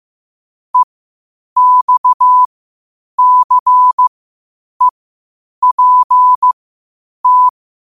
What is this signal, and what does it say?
Morse code, heard as a keyed tone, about 15 wpm, spaced as for 8 wpm, 1000 Hz. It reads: EXCEPT
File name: qotdfiles / 8wpm